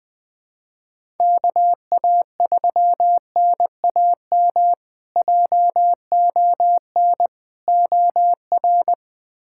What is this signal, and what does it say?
Morse code, heard as a keyed tone, about 20 wpm, 700 Hz. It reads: KA3NAM JON OR
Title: K1USN SST QSO Practice - 20wpm